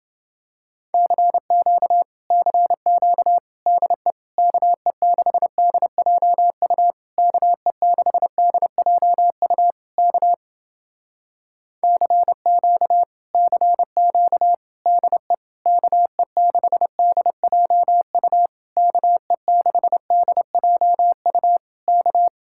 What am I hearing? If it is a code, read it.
Morse code, heard as a keyed tone, about 29 wpm, 700 Hz. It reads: CQ CQ DE KE6DJU KE6DJU K CQ CQ DE KE6DJU KE6DJU K